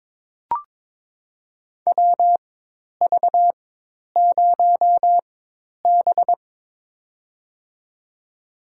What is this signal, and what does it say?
Morse code, heard as a keyed tone, about 22 wpm, spaced as for 10 wpm, 700 Hz. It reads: WV0B
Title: USA Call Signs - 4 Characters - 4x Character Spacing - 22wpm